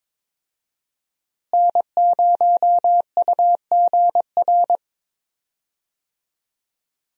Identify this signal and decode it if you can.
Morse code, heard as a keyed tone, about 22 wpm, 700 Hz. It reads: N0UGR
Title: USA Call Signs - 5 Characters - Repeated 3x - 22wpm